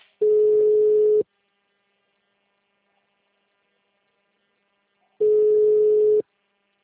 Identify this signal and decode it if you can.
Ringback tone (Europe, 425 Hz).